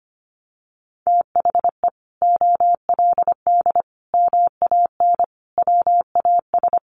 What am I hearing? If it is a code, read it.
Morse code, heard as a keyed tone, about 25 wpm, 700 Hz. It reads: THE OLD MAN WAS
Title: Sentences from Top 1100 Words 25wpm